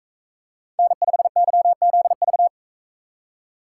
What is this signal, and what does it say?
Morse code, heard as a keyed tone, about 45 wpm, 700 Hz. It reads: N5YZV